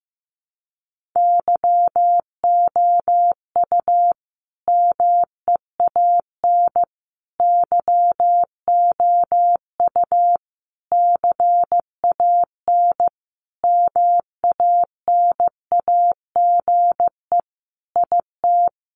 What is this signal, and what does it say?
Morse code, heard as a keyed tone, about 15 wpm, 700 Hz. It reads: YOU MEAN YOU CAN MANAGE IT
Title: Sentences from Top 1400 Words 15wpm